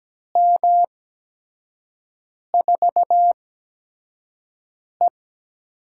Morse code, heard as a keyed tone, about 17 wpm, spaced as for 5 wpm, 700 Hz. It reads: M4E
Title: USA Call Signs - 4 Characters - 8x Character Spacing - 17wpm